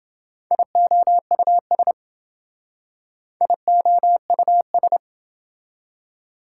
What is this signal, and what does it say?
Morse code, heard as a keyed tone, about 26 wpm, 700 Hz. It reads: IOUS IOUS